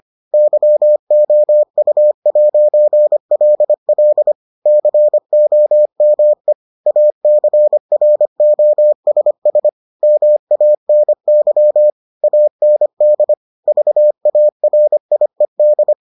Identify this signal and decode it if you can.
Morse code, heard as a keyed tone, about 25 wpm, 600 Hz. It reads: YOU'LL COME ACROSS MANY AND VARIED